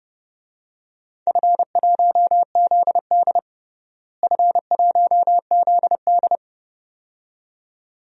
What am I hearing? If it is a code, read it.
Morse code, heard as a keyed tone, about 30 wpm, 700 Hz. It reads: F1ZD F1ZD
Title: DXCC-100 Call Signs - 4 Characters - Repeated 2x - 30wpm